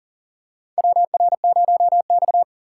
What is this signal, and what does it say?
Morse code, heard as a keyed tone, about 37 wpm, 700 Hz. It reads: WR0X
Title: USA Call Signs - 4 Characters - 3x Character Spacing - 40wpm